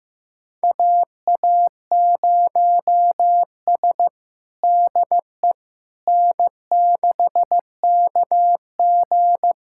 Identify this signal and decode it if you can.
Morse code, heard as a keyed tone, about 15 wpm, 700 Hz. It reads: AA0S DE N6KG
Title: USA Callsign DE Callsign - 4 Characters - 15wpm